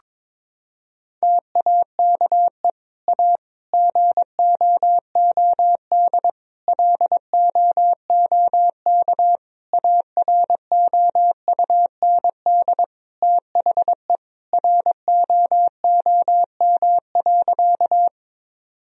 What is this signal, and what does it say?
Morse code, heard as a keyed tone, about 22 wpm, 700 Hz. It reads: TAKE A GOOD LOOK AROUND THE ROOM.
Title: Sentences from Top 1000 Words - Review Part 3 - 22wpm